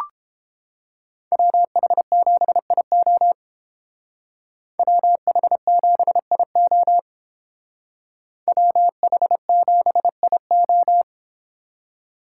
Morse code, heard as a keyed tone, about 31 wpm, 700 Hz. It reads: WH7IO WH7IO WH7IO